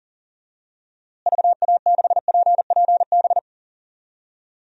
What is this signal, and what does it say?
Morse code, heard as a keyed tone, about 38 wpm, 700 Hz. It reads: VA6PPB
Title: DXCC-100 Call Signs - 6 Characters - Repeated 2x - 40wpm